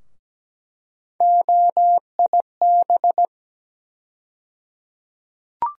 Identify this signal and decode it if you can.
Morse code, heard as a keyed tone, about 17 wpm, 700 Hz. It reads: OIB
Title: USA Call Sign Postfixes - 3 Letters 17wpm